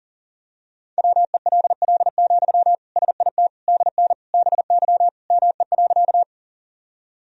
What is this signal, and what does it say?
Morse code, heard as a keyed tone, about 40 wpm, 700 Hz. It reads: WELL, SIT DN BY ME.
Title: Sentences from Top 500 Words and Abbreviations 40wpm